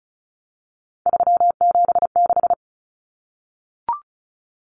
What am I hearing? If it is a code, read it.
Morse code, heard as a keyed tone, about 34 wpm, 700 Hz. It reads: 376
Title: Common Cut Numbers - 3-Digit 35wpm